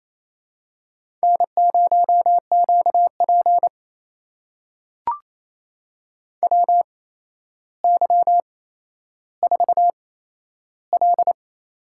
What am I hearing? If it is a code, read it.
Morse code, heard as a keyed tone, about 28 wpm, spaced as for 12 wpm, 700 Hz. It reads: N0QP WY4L
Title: USA Call Signs - 4 Characters - 8x Character Spacing - 28wpm